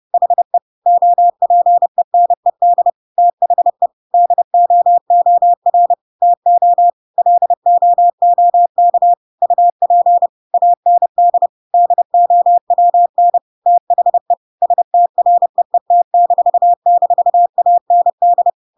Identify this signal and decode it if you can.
Morse code, heard as a keyed tone, about 30 wpm, 700 Hz. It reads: HE OPENED THE DOOR TO LOOK UP AND DOWN THE STREET--AND
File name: Pinocchio / Pinocchio ch1 30 30 ik1hge